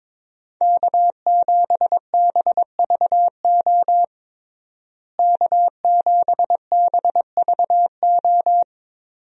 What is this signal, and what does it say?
Morse code, heard as a keyed tone, about 22 wpm, 700 Hz. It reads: K7BVO K7BVO
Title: USA Call Signs - 5 Characters - Repeated 2x - 22wpm